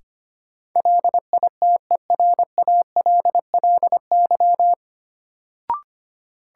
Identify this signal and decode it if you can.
Morse code, heard as a keyed tone, about 25 wpm, 700 Hz. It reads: LITERALLY
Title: Sentences from Top 1400 Words - New Word Drill - 25wpm